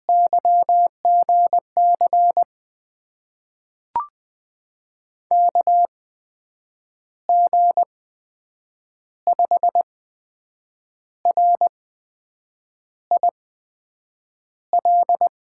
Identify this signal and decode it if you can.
Morse code, heard as a keyed tone, about 20 wpm, spaced as for 7 wpm, 700 Hz. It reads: YGC KG5RIL
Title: USA Call Signs - 6 Characters - 8x Character Spacing - 20wpm